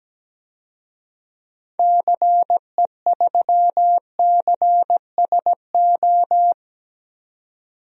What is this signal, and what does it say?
Morse code, heard as a keyed tone, about 17 wpm, 700 Hz. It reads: CE3CSO
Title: DXCC-100 Call Signs - 6 Characters - 17wpm